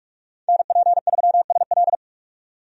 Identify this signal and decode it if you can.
Morse code, heard as a keyed tone, about 44 wpm, 700 Hz. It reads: NP3SL
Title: USA Call Signs - 5 Characters - 45wpm